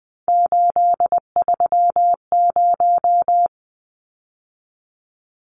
Morse code, heard as a keyed tone, about 20 wpm, 700 Hz. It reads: 830
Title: Numbers - 3-Digit 20wpm